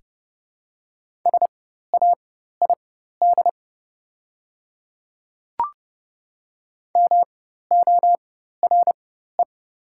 Morse code, heard as a keyed tone, about 30 wpm, spaced as for 14 wpm, 700 Hz. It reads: SAID MORE
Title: 4x Character Spacing - Top 200 Words - 30wpm